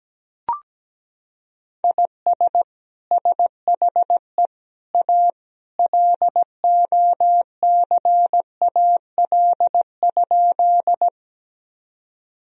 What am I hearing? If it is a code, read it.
Morse code, heard as a keyed tone, about 17 wpm, 700 Hz. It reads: IS SHE A LOCAL?
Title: Sentences from Top 1000 Words - Review Part 3 - 17wpm